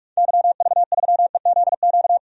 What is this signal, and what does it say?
Morse code, heard as a keyed tone, about 43 wpm, 700 Hz. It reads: YV3EZQ
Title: DXCC-100 Call Signs - 6 Characters - 45wpm